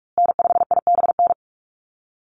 Morse code, heard as a keyed tone, about 44 wpm, 700 Hz. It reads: N5IBN
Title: USA Call Signs Repeated 2x 45wpm